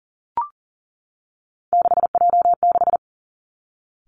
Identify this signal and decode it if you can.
Morse code, heard as a keyed tone, about 38 wpm, 700 Hz. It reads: 6J6
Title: Three Challenging Letter-Number Combinations 40wpm